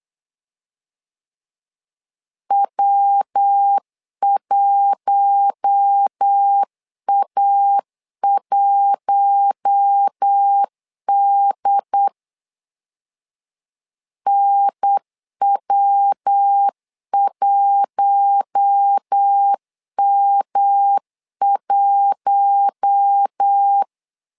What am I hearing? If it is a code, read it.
Morse code, heard as a keyed tone, about 8 wpm, 800 Hz. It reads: W1A1D NW1M1
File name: LEZIONI / 4 Q1 8wpm 3035